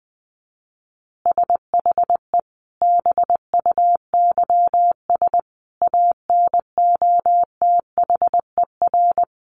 Morse code, heard as a keyed tone, about 20 wpm, 700 Hz. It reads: SHE BUYS ANOTHER
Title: Easy Phrases 20wpm